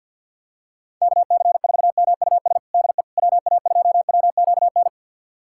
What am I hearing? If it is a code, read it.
Morse code, heard as a keyed tone, about 52 wpm, 700 Hz. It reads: KC4MUS DE WA2WQN
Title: USA Callsign DE Callsign 50wpm